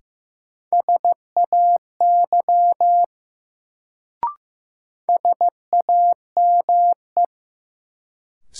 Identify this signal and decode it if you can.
Morse code, heard as a keyed tone, about 15 wpm, 700 Hz. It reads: SAY SAME
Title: Sentences from Top 200 Words - New Word Drill - 15wpm